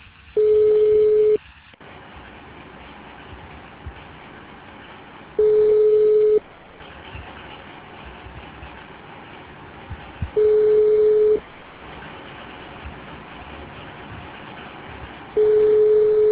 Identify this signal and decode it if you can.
Ringback tone (Europe, 425 Hz).